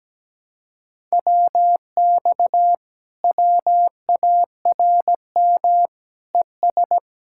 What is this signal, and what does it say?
Morse code, heard as a keyed tone, about 17 wpm, 700 Hz. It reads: WX WARM ES
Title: Speed-Racing Common QSO Phrases - 17wpm